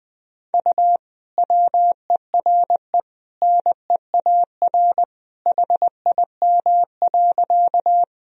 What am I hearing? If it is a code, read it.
Morse code, heard as a keyed tone, about 20 wpm, 700 Hz. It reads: U WERE NEAR HIM.
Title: Sentences from Top 200 Words and Abbreviations 20wpm